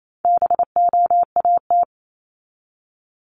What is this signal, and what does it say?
Morse code, heard as a keyed tone, about 28 wpm, 700 Hz. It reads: BOAT
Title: Top 500 Words 28wpm